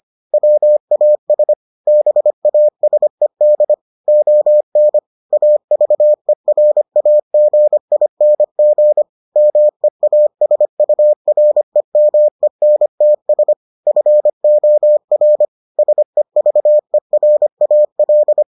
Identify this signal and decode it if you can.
Morse code, heard as a keyed tone, about 25 wpm, 600 Hz. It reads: WAS BASED ON AVERAGING MEASUREMENTS FOR SEVERAL